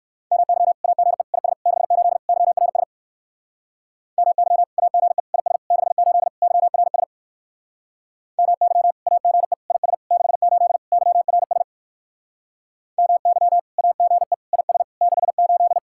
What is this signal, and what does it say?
Morse code, heard as a keyed tone, about 59 wpm, 700 Hz. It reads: MY AGE IS 68 YRS MY AGE IS 68 YRS MY AGE IS 68 YRS MY AGE IS 68